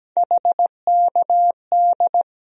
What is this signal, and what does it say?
Morse code, heard as a keyed tone, about 17 wpm, 700 Hz. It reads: HKD
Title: USA Call Sign Postfixes - 3 Letters 17wpm